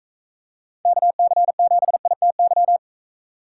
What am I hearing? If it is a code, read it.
Morse code, heard as a keyed tone, about 34 wpm, 700 Hz. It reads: KC7ITY
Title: Speed Racing - USA 2x3 Call Signs 28wpm